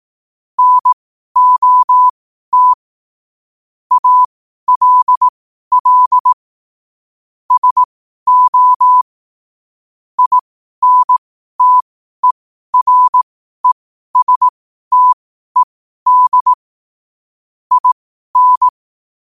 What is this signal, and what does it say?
Morse code, heard as a keyed tone, about 18 wpm, spaced as for 12 wpm, 1000 Hz. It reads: NOT ALL SO INTERESTED IN
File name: qotdfiles / 12wpm